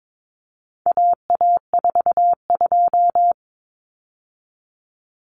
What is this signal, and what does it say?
Morse code, heard as a keyed tone, about 22 wpm, 700 Hz. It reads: AA42
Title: Common Cut Numbers - 4-Digit 22wpm